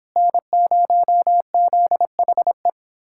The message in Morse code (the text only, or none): N0ZHE